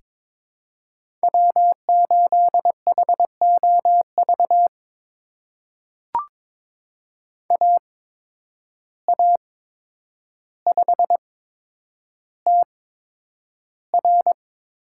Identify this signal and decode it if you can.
Morse code, heard as a keyed tone, about 22 wpm, spaced as for 9 wpm, 700 Hz. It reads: W8HOV AA5TR
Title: USA Call Signs - 5 Characters - 8x Character Spacing - 22wpm